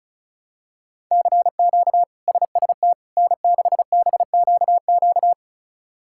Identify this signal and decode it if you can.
Morse code, heard as a keyed tone, about 34 wpm, 700 Hz. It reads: CQ SST N6BQQ
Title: K1USN SST QSO Practice - 35wpm